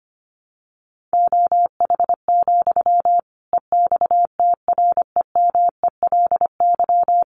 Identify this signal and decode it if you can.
Morse code, heard as a keyed tone, about 25 wpm, 700 Hz. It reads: OH, EXTREMELY
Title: Sentences from Top 1400 Words 25wpm